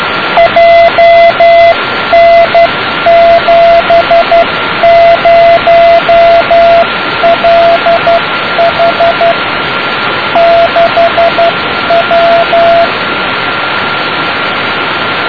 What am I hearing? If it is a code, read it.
Morse code, heard as a keyed tone, about 12 wpm, 680 Hz, with radio noise. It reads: JN70LH 6W
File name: iz8dxb beacon 0653 28242